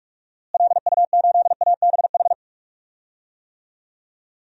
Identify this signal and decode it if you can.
Morse code, heard as a keyed tone, about 43 wpm, 700 Hz. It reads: LU8ABH
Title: DXCC-100 Call Signs - 6 Characters - Repeated 2x - 45wpm